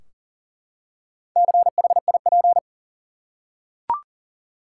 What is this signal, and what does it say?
Morse code, heard as a keyed tone, about 40 wpm, 700 Hz. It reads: CHIP
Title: Four Letter Words - 40wpm